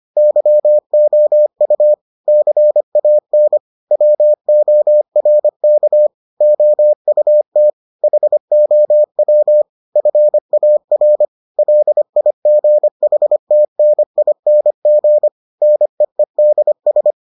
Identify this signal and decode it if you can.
Morse code, heard as a keyed tone, about 25 wpm, 600 Hz. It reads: YOU CAN WORK OUT HOW FAR LIGHTNING NEEDS